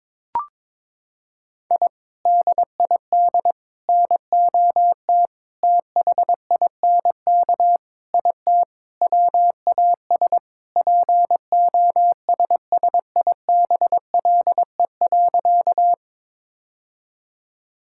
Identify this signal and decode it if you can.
Morse code, heard as a keyed tone, about 22 wpm, 700 Hz. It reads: I DID NOT THINK IT WAS POSSIBLE.
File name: practice / 22wpm / Sentences with 4-Word Phrases - 22wpm